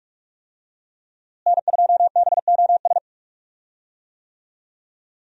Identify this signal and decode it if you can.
Morse code, heard as a keyed tone, about 42 wpm, 700 Hz. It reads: N2BOS
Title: USA Call Signs - 5 Characters - Repeated 3x - 45wpm